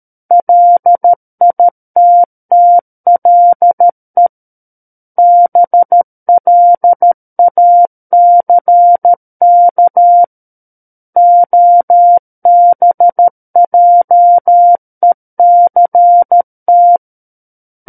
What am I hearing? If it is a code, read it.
Morse code, heard as a keyed tone, about 13 wpm, 700 Hz. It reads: LITTLE BLACK OBJECT
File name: Book One / 04-Chapter 4 - 13 WPM